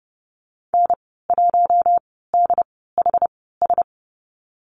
Morse code, heard as a keyed tone, about 30 wpm, spaced as for 17 wpm, 700 Hz. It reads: N1DHS
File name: Lesson 15g - Number 5 - Call Signs 30wpm